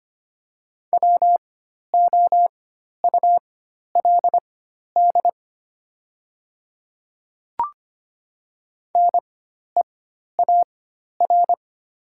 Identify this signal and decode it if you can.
Morse code, heard as a keyed tone, about 25 wpm, spaced as for 12 wpm, 700 Hz. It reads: WOULD NEAR
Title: 4x Character Spacing - Top 200 Words - 25wpm